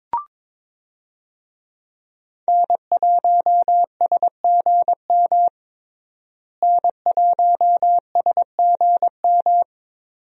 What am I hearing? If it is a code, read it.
Morse code, heard as a keyed tone, about 22 wpm, 700 Hz. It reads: N1SGM N1SGM
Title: USA Call Signs - 5 Characters - Repeated 2x - 22wpm